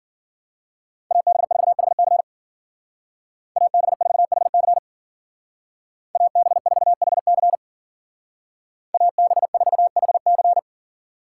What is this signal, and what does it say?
Morse code, heard as a keyed tone, about 49 wpm, 700 Hz. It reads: AB4HC AB4HC AB4HC AB4HC
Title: Speed Racing - USA 2x2 Call Signs 40wpm